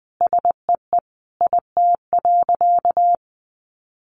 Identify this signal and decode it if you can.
Morse code, heard as a keyed tone, about 20 wpm, 700 Hz. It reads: SEE IT.
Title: A Sound Of Thunder 20wpm